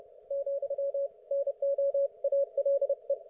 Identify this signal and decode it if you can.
Morse code, heard as a keyed tone, about 30 wpm, 560 Hz. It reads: , NO ALE
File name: Archiv / CUC208 / ok1fhi cul208